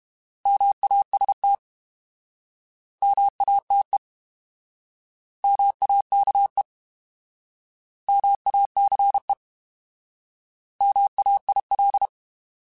Morse code, heard as a keyed tone, about 32 wpm, 800 Hz. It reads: MAST MATE MAKE MACE MAIL
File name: Alphabetical / Alphabetical mp3s / Ms @30